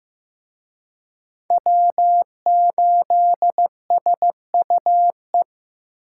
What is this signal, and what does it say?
Morse code, heard as a keyed tone, about 18 wpm, 700 Hz. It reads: W8SUE
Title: Speed Racing - USA Call Signs 15wpm